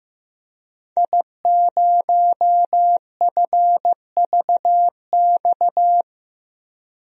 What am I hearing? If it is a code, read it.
Morse code, heard as a keyed tone, about 15 wpm, 700 Hz. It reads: I0FVX
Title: DXCC-100 Call Signs - 5 Characters - 15wpm